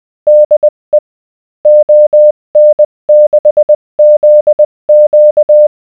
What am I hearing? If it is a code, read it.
Morse code, heard as a keyed tone, about 20 wpm, spaced as for 18 wpm, 600 Hz. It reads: DE ON6ZQ